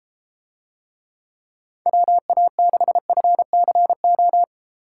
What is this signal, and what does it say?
Morse code, heard as a keyed tone, about 33 wpm, 700 Hz. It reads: WA6FCO